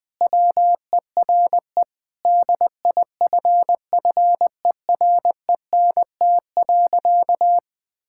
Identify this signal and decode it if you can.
Morse code, heard as a keyed tone, about 20 wpm, 700 Hz. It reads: WERE DIFFERENT.
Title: Sentences with 5-Word Phrases - Spoken Beginning - 20wpm